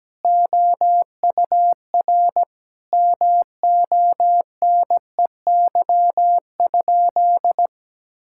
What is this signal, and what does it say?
Morse code, heard as a keyed tone, about 17 wpm, 700 Hz. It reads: OUR MONEY?
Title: Sentences from Top 1000 Words - Review Part 5 - 17wpm